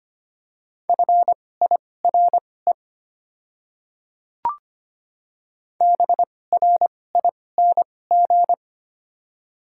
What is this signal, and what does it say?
Morse code, heard as a keyed tone, about 25 wpm, spaced as for 18 wpm, 700 Hz. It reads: FIRE BRING
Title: 2x Character Spacing - Top 500 Words - 25wpm